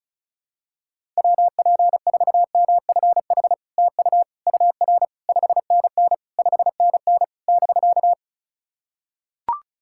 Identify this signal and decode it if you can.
Morse code, heard as a keyed tone, about 36 wpm, 700 Hz. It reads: WP4MFH TU UR 5NN 5NN <BK>
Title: SOTA QSO Practice 35wpm